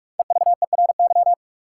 Morse code, heard as a keyed tone, about 43 wpm, 700 Hz. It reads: EVERY
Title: Sets of 5 Words - New Word Drill - 45wpm